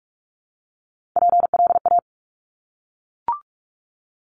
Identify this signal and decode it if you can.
Morse code, heard as a keyed tone, about 46 wpm, 700 Hz. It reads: PLA
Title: Top Three Letter Combinations in Words 45wpm